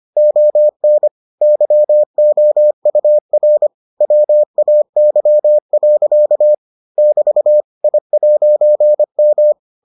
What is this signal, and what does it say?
Morse code, heard as a keyed tone, about 25 wpm, 600 Hz. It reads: ON YOUR WAY. <BT> I'M